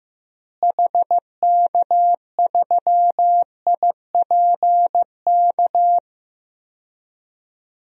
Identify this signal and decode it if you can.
Morse code, heard as a keyed tone, about 15 wpm, 700 Hz. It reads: HK3IPK
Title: DXCC-100 Call Signs - 6 Characters - Repeated 3x - 15wpm